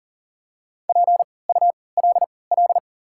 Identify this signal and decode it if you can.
Morse code, heard as a keyed tone, about 40 wpm, spaced as for 23 wpm, 700 Hz. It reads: PULL
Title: 3x Character Spacing - Top 500 Words - 40wpm